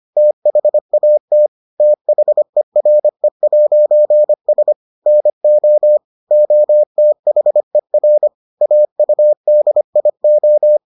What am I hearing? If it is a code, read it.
Morse code, heard as a keyed tone, about 25 wpm, 600 Hz. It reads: THAT THERE'S NO OTHER AUDIO